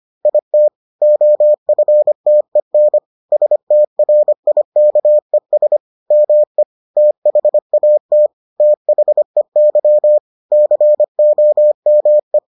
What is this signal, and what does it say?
Morse code, heard as a keyed tone, about 25 wpm, 600 Hz. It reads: IT OFTEN STRIKES ME THAT THEY COME